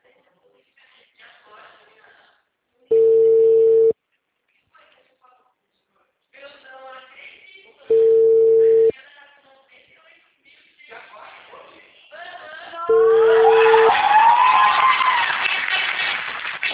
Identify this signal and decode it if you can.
Ringback tone (China, 450 Hz), with speech.